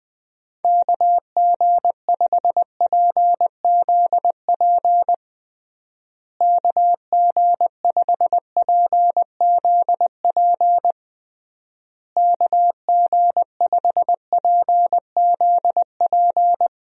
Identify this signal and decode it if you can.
Morse code, heard as a keyed tone, about 20 wpm, 700 Hz. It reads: KG5PZP KG5PZP KG5PZP
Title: USA Call Signs - 6 Characters - Repeated 3x - 20wpm